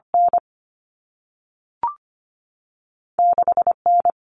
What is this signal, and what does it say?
Morse code, heard as a keyed tone, about 25 wpm, 700 Hz. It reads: N 6 N
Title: Common Cut Numbers - 2-Digit 25wpm